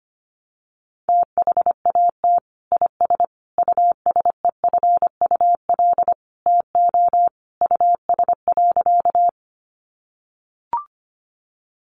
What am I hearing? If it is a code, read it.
Morse code, heard as a keyed tone, about 25 wpm, 700 Hz. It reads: THAT IS USEFUL TO US.
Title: Sentences from Top 700 Words 25wpm